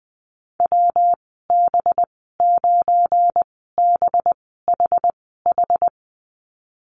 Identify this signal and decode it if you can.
Morse code, heard as a keyed tone, about 19 wpm, spaced as for 14 wpm, 700 Hz. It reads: WB9BHH
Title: 213 Lesson 34g - Number 8 - Call Signs 20wpm